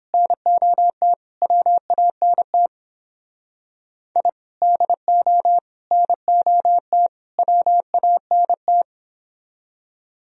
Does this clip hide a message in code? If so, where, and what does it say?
Yes: Morse code from 0.1 to 8.8 s, heard as a keyed tone, about 31 wpm, 700 Hz. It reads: NOT WANT I DO NOT WANT